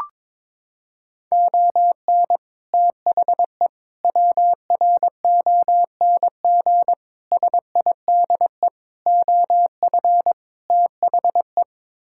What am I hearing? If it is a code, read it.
Morse code, heard as a keyed tone, about 22 wpm, 700 Hz. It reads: ON THE WRONG SIDE OF THE
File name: practice / 22wpm / Sets of 6 Words - 22wpm